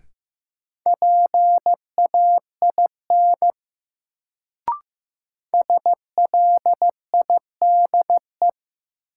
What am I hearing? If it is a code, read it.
Morse code, heard as a keyed tone, about 15 wpm, 700 Hz. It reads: PAIN SLIDE
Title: Sentences from Top 1600 Words - New Word Drill Alt - 15wpm